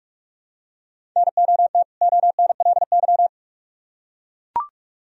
Morse code, heard as a keyed tone, about 44 wpm, 700 Hz. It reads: NOT ONLY